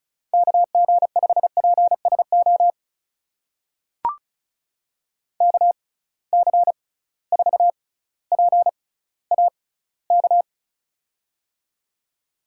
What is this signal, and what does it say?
Morse code, heard as a keyed tone, about 36 wpm, spaced as for 18 wpm, 700 Hz. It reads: KG5PSO KC4PAK
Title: USA Call Signs - 6 Characters - 6x Character Spacing - 35wpm